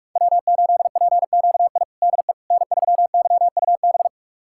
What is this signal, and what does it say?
Morse code, heard as a keyed tone, about 46 wpm, 700 Hz. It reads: W8PQI DE N3YUB
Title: USA Callsign DE Callsign - 5 Characters - 45wpm